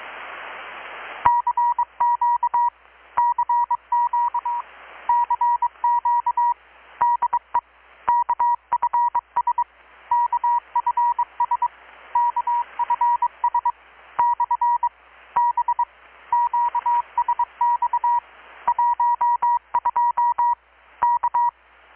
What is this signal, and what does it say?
Morse code, heard as a keyed tone, about 22 wpm, 960 Hz. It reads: CQ CQ CQ DE KFS KFS KFS / B QSX 12 K
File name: blog / swl / 6-abril-2019-12695-USB-KPS